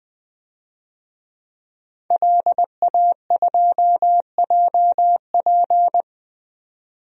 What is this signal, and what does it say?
Morse code, heard as a keyed tone, about 21 wpm, 700 Hz. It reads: LA2JP
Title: DXCC-100 Call Signs - 5 Characters - 20wpm